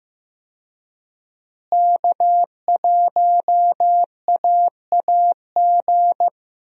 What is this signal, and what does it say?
Morse code, heard as a keyed tone, about 15 wpm, 700 Hz. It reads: K1AAG
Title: USA Call Signs - 5 Characters - Repeated 3x - 15wpm